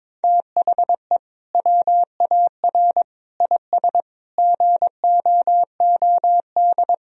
Morse code, heard as a keyed tone, about 22 wpm, 700 Hz. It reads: THE WAR IS GOOD
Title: Sentences from Top 1000 Words - Review Part 4 - 22wpm